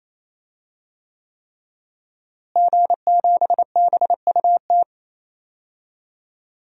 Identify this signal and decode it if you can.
Morse code, heard as a keyed tone, about 28 wpm, 700 Hz. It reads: G7BUT
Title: DXCC-100 Call Signs - 5 Characters - Repeated 3x - 28wpm